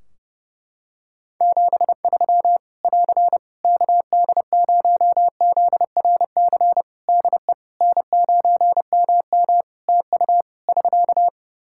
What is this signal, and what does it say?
Morse code, heard as a keyed tone, about 30 wpm, 700 Hz. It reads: 73 <AR> KD0ZRC DE N9MM TU <SK>
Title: Practice QSO - Exchange 4 - Ending - 30wpm